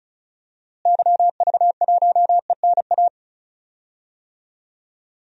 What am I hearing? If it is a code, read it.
Morse code, heard as a keyed tone, about 34 wpm, 700 Hz. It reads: YV1ENA
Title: DXCC-100 Call Signs - 6 Characters - Repeated 2x - 35wpm